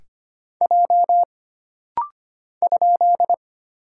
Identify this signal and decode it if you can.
Morse code, heard as a keyed tone, about 25 wpm, 700 Hz. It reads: J ?
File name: Single Character (Letter:Number:Punctuation:Prosign) - Rapid-Fire - 25wpm